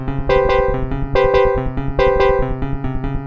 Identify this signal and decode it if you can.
There is no code.